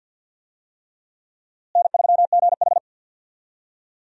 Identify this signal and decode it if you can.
Morse code, heard as a keyed tone, about 48 wpm, 700 Hz. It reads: N3GH
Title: USA Call Signs - 4 Characters - Repeated 3x - 50wpm